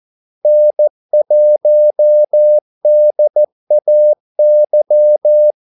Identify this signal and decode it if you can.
Morse code, heard as a keyed tone, about 14 wpm, 600 Hz. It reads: N1DAY
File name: Callsigns-14wpm